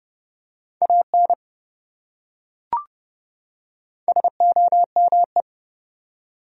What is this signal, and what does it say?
Morse code, heard as a keyed tone, about 27 wpm, 700 Hz. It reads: AN SOME